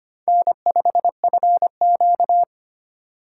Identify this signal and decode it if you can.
Morse code, heard as a keyed tone, about 25 wpm, 700 Hz. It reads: N5FQ
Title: USA Call Signs - 4 Characters - 4x Character Spacing - 25wpm